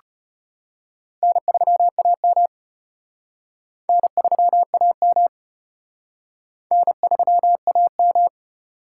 Morse code, heard as a keyed tone, about 32 wpm, 700 Hz. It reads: N3AM N3AM N3AM